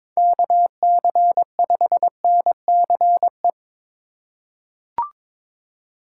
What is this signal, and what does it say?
Morse code, heard as a keyed tone, about 21 wpm, 700 Hz. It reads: KC5NCE
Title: USA Call Signs - 6 Characters - 4x Character Spacing - 22wpm